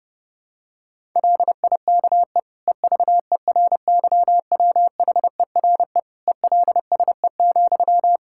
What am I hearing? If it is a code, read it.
Morse code, heard as a keyed tone, about 32 wpm, 700 Hz. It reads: LIKE EVERYWHERE ELSE,
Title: Sentences from Top 1600 Words 30wpm